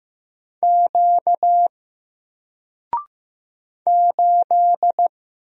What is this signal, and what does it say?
Morse code, heard as a keyed tone, about 15 wpm, 700 Hz. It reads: Q 8
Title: Single Character (Letter:Number:Punctuation:Prosign) 15wpm